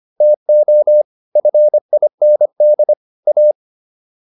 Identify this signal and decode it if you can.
Morse code, heard as a keyed tone, about 25 wpm, 600 Hz. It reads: TO FIND A